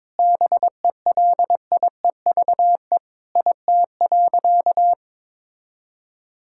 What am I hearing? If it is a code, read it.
Morse code, heard as a keyed tone, about 22 wpm, 700 Hz. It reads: BELIEVE IT.
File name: practice / 22wpm / Sentences with 3-Word Phrases - 22wpm